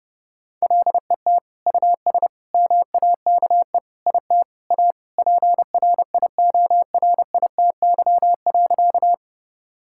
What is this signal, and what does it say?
Morse code, heard as a keyed tone, about 30 wpm, 700 Hz. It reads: LET US MAKE IT A PRIORITY.
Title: Sentences from Top 1600 Words 30wpm